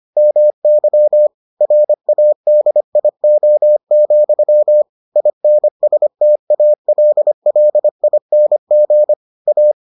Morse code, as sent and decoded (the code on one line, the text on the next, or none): -- -.-- / .-. .- -.. .. --- --..-- / .. -. ... - .- .-.. .-.. .. -. --. / .-
MY RADIO, INSTALLING A